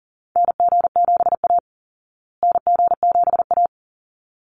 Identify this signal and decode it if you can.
Morse code, heard as a keyed tone, about 38 wpm, 700 Hz. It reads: NG7A NG7A
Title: Call Signs Repeated 3x - 40wpm